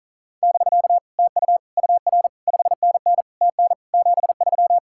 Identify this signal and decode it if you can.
Morse code, heard as a keyed tone, about 42 wpm, 700 Hz. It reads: <BK> TU UR 5NN TN 73
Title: Speed-Racing Common QSO Phrases Encore - 35wpm